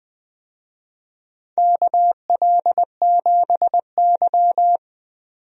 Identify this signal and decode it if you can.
Morse code, heard as a keyed tone, about 20 wpm, 700 Hz. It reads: KL7Y
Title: DXCC-100 Call Signs - 4 Characters - 20wpm